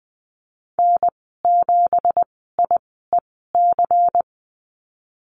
Morse code, heard as a keyed tone, about 20 wpm, spaced as for 15 wpm, 700 Hz. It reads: N7IEC
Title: 173 Lesson 28g - Number 9 - Call Signs 20wpm